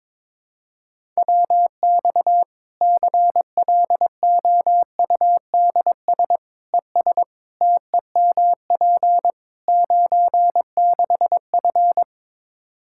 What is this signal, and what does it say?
Morse code, heard as a keyed tone, about 22 wpm, 700 Hz. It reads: WX CLOUDS ES TEMP 96F
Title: Practice QSO - Exchange 2 - Info - 22wpm